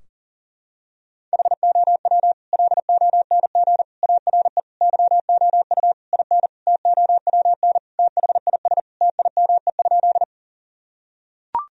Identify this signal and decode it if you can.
Morse code, heard as a keyed tone, about 40 wpm, 700 Hz. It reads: HOW LONG ARE YOU IN TOWN THIS TIME?
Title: Sentences from Top 1000 Words - Review Part 2 - 40wpm